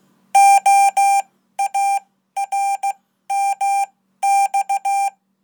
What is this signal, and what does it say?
Morse code, heard as a keyed tone, about 15 wpm, spaced as for 12 wpm, 790 Hz. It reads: OARMX
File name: Code